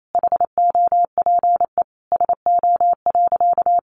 Morse code, heard as a keyed tone, about 28 wpm, 700 Hz. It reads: HOPE SO.
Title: Sentences from Top 1000 Words 28wpm